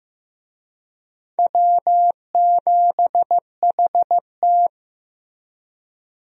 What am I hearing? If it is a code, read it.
Morse code, heard as a keyed tone, about 15 wpm, 700 Hz. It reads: W7HT
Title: USA Call Signs - 4 Characters - Repeated 3x - 15wpm